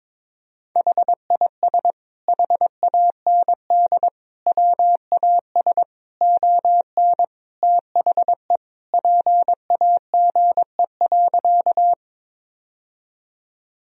Morse code, heard as a keyed tone, about 23 wpm, 700 Hz. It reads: HIS HAND WAS ON THE PAGE.
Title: Sentences from Top 1000 Words - Review Part 1 - 22wpm